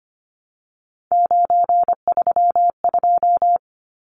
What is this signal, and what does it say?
Morse code, heard as a keyed tone, about 25 wpm, 690 Hz. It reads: 932
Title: Numbers - 3-Digit 25wpm